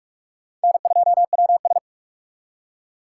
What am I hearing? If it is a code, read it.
Morse code, heard as a keyed tone, about 60 wpm, spaced as for 52 wpm, 700 Hz. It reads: N2WS
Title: Speed Racing - USA 1x2 Call Signs 45wpm